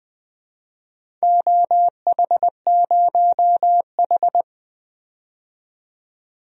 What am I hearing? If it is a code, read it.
Morse code, heard as a keyed tone, about 20 wpm, 700 Hz. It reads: OH0H
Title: DXCC-100 Call Signs - 4 Characters - 20wpm